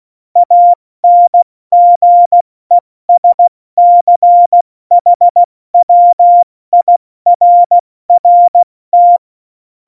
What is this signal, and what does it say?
Morse code, heard as a keyed tone, about 16 wpm, spaced as for 14 wpm, 700 Hz. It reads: ANGESCHWIRRT
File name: Morsekurs / Witze / Witz 25. September 2018, 70 F80